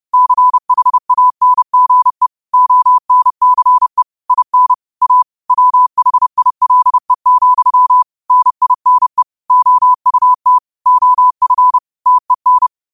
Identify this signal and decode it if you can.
Morse code, heard as a keyed tone, about 30 wpm, 1000 Hz. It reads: CHANGE ONCE IN A WHILE, NINE OUT OF TEN
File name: 30wpm